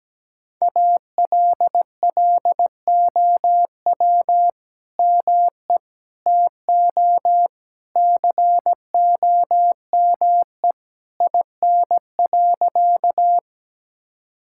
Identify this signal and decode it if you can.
Morse code, heard as a keyed tone, about 17 wpm, 700 Hz. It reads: ALLOW ME TO COME IN.
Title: Sentences from Top 1000 Words - Review Part 5 - 17wpm